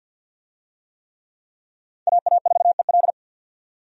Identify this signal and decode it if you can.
Morse code, heard as a keyed tone, about 47 wpm, 700 Hz. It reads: AA4EL
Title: USA Call Signs - 5 Characters - 50wpm